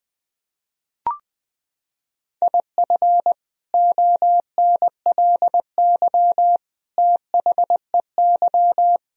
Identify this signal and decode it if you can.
Morse code, heard as a keyed tone, about 20 wpm, 700 Hz. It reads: IF ONLY THEY
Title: Sentences from Top 500 Words and Abbreviations 20wpm